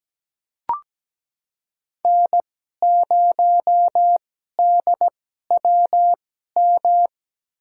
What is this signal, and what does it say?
Morse code, heard as a keyed tone, about 17 wpm, spaced as for 12 wpm, 700 Hz. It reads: N0DWM
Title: USA Call Signs - 5 Characters - 2x Character Spacing - 17wpm